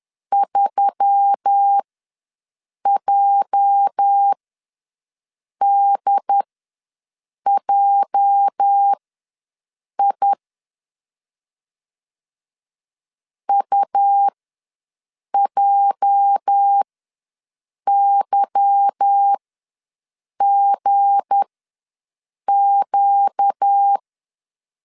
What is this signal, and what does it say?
Morse code, heard as a keyed tone, about 10 wpm, spaced as for 6 wpm, 800 Hz. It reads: 3JDJI UJYGQ
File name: LEZIONI / 8 YJ45 / 8 YJ ws 10wpm 4043